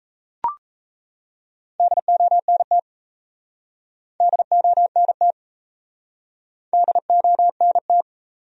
Morse code, heard as a keyed tone, about 38 wpm, 700 Hz. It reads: DONT DONT DONT